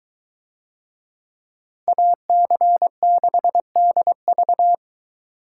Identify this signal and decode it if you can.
Morse code, heard as a keyed tone, about 22 wpm, 700 Hz. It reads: AC6DV